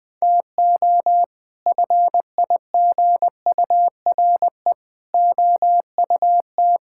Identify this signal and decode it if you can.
Morse code, heard as a keyed tone, about 20 wpm, 700 Hz. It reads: TO FIGURE OUT
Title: Sets of 6 Words - 20wpm